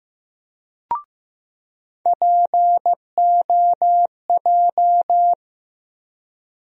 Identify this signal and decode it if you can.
Morse code, heard as a keyed tone, about 15 wpm, 700 Hz. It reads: POJ
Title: USA Call Sign Postfixes - 3 Letters 15wpm